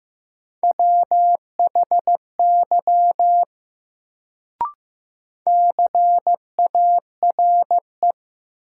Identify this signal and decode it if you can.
Morse code, heard as a keyed tone, about 15 wpm, 700 Hz. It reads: WHY CARE
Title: Sets of 3 Words - New Word Drill - 15wpm